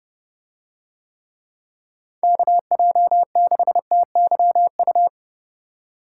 Morse code, heard as a keyed tone, about 27 wpm, 700 Hz. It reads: KJ6TYU